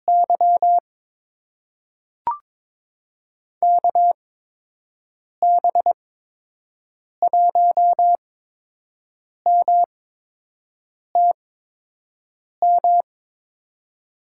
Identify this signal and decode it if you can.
Morse code, heard as a keyed tone, about 22 wpm, spaced as for 6 wpm, 700 Hz. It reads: Y KB1MTM
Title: USA Call Signs - 6 Characters - 8x Character Spacing - 22wpm